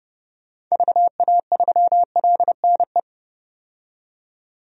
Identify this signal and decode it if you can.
Morse code, heard as a keyed tone, about 30 wpm, 700 Hz. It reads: VA3LNE
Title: DXCC-100 Call Signs - 6 Characters - 30wpm